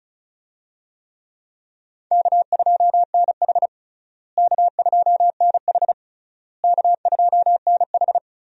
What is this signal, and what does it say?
Morse code, heard as a keyed tone, about 34 wpm, 700 Hz. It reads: K2NH K2NH K2NH
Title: USA Call Signs - 4 Characters - Repeated 3x - 35wpm